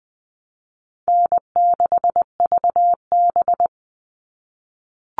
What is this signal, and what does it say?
Morse code, heard as a keyed tone, about 20 wpm, 700 Hz. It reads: N6VB
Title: All Cut Numbers - 4-Digit 20wpm